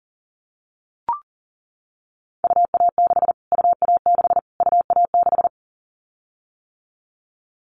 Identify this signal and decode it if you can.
Morse code, heard as a keyed tone, about 40 wpm, 700 Hz. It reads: UA6 UA6 UA6
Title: Callsign Prefixes for Top 25 DXCC - Repeated 3x - 40wpm